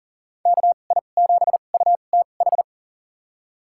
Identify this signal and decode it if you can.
Morse code, heard as a keyed tone, about 40 wpm, spaced as for 29 wpm, 700 Hz. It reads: KI7UTH
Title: USA Call Signs - 6 Characters - 2x Character Spacing - 40wpm